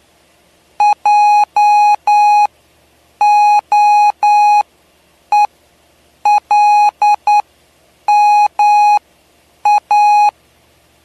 Morse code, heard as a keyed tone, about 9 wpm, spaced as for 7 wpm, 830 Hz. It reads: JOELMA
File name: cw / py2lcd / 07-PY2LCD-Li��o 7